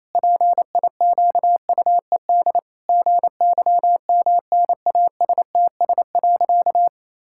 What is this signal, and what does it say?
Morse code, heard as a keyed tone, about 28 wpm, 700 Hz. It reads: PIQUED GYMNASTS.
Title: Pangram Challenge v2 - 28wpm